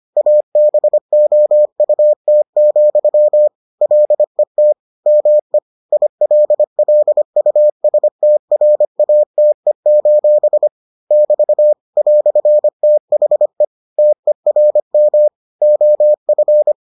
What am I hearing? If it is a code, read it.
Morse code, heard as a keyed tone, about 25 wpm, 600 Hz. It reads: ABOUT, LET ME ILLUSTRATE: <BT> "THE TERM OF